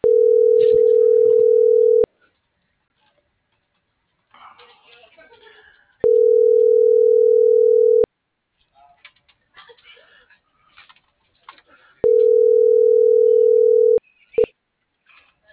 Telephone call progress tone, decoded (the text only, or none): ringback tone